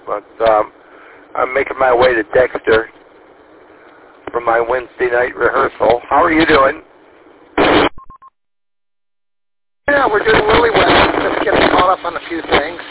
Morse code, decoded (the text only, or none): none